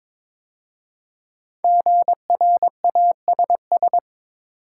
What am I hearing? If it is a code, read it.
Morse code, heard as a keyed tone, about 22 wpm, 700 Hz. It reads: GRASS
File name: 5 Letter Words - First 3 Letters Given - 22wpm